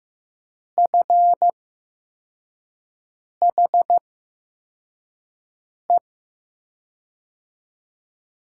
Morse code, heard as a keyed tone, about 15 wpm, spaced as for 4 wpm, 700 Hz. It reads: FHE
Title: USA Call Signs - 6 Characters - 8x Character Spacing - 15wpm